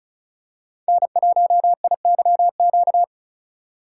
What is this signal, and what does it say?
Morse code, heard as a keyed tone, about 34 wpm, 700 Hz. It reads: N1IYQ
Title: USA Call Signs - 5 Characters - 2x Character Spacing - 35wpm